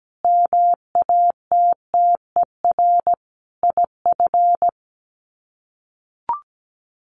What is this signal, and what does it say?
Morse code, heard as a keyed tone, about 17 wpm, 700 Hz. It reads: MATTER IF
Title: Sets of 5 Words Encore 17wpm